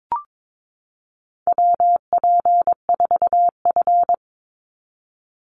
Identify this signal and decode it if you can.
Morse code, heard as a keyed tone, about 22 wpm, 700 Hz. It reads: WP4F
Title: Call Signs Repeated 2x - 22wpm